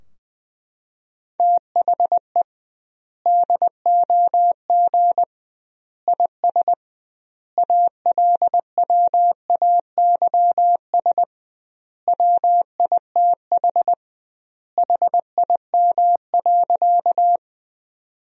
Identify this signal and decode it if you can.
Morse code, heard as a keyed tone, about 20 wpm, 700 Hz. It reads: THE DOG IS ALWAYS WITH HIM.
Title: 2x Word-Spacing - Sentences from Top 400 Words - 20wpm